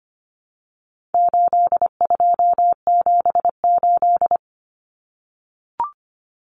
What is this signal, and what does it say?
Morse code, heard as a keyed tone, about 25 wpm, 700 Hz. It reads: 8278
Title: Numbers - 4-Digit 25wpm